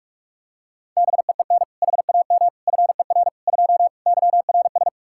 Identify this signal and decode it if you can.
Morse code, heard as a keyed tone, about 45 wpm, 700 Hz. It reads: BEEN HAM FER 2 YRS
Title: K1USN SST QSO Practice 45wpm